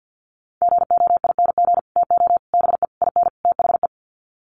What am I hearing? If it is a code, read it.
Morse code, heard as a keyed tone, about 50 wpm, 700 Hz. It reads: GOING TO BE IN THE
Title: Sets of 5 Words Encore 50wpm